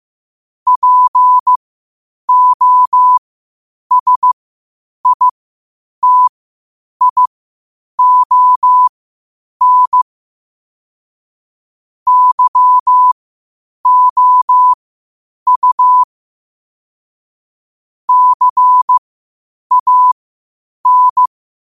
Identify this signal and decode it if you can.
Morse code, heard as a keyed tone, about 15 wpm, spaced as for 8 wpm, 1000 Hz. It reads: POSITION YOU CAN
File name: qotdfiles / 8wpm